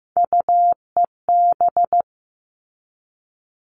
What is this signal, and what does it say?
Morse code, heard as a keyed tone, about 15 wpm, 700 Hz. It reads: UEB
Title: All Cut Numbers - 3-Digit 15wpm